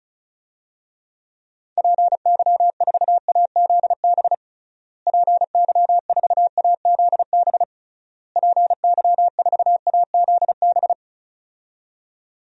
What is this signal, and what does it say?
Morse code, heard as a keyed tone, about 34 wpm, 700 Hz. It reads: PY4AZB PY4AZB PY4AZB
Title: DXCC-100 Call Signs - 6 Characters - Repeated 3x - 35wpm